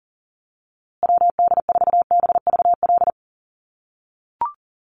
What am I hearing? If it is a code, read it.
Morse code, heard as a keyed tone, about 40 wpm, 700 Hz. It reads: WD4BVL
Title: USA Call Signs Repeated 2x 40wpm